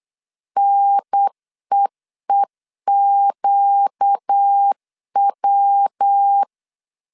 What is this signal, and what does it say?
Morse code, heard as a keyed tone, about 8 wpm, 800 Hz. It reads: NEEQW
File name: CW / DET / 1 6 1500C S B 8wpm 5142